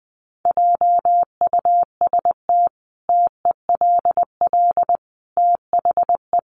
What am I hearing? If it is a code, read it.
Morse code, heard as a keyed tone, about 19 wpm, 700 Hz. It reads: JUST TELL THE
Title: Sentences from Top 800 Words 20wpm